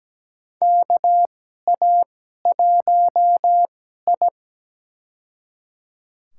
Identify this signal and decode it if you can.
Morse code, heard as a keyed tone, about 17 wpm, spaced as for 12 wpm, 700 Hz. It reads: KA1I
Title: USA Call Signs - 2x Character Spacing - 17wpm